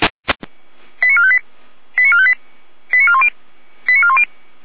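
Five-tone selective call (ZVEI), heard as quick runs of five tones, tones of about 70 ms. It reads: 89458 89458 89410 89410
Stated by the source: ZVEI five-tone selective call 88176 received by SDR, logged on the feuerwehr-orlamuende.org audio server.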